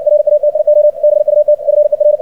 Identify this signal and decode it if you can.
Morse code, heard as a keyed tone, about 60 wpm, 600 Hz. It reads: RATIO BUT FER
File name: archived / CRSnet